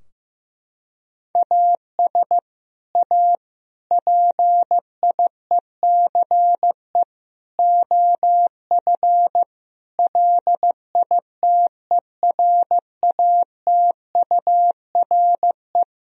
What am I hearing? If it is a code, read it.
Morse code, heard as a keyed tone, about 15 wpm, 700 Hz. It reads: AS A PIECE OF LITERATURE
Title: Sentences from Top 1000 Words - Review Part 3 - 15wpm